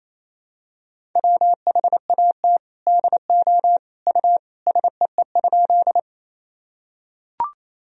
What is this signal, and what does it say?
Morse code, heard as a keyed tone, about 28 wpm, 700 Hz. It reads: WHAT DO U SEE?
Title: Sentences from Top 100 Words and Abbreviations 28wpm